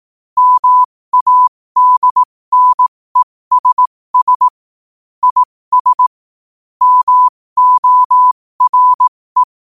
Morse code, heard as a keyed tone, about 18 wpm, spaced as for 15 wpm, 1000 Hz. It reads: MADNESS IS MORE
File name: qotdfiles / 15wpm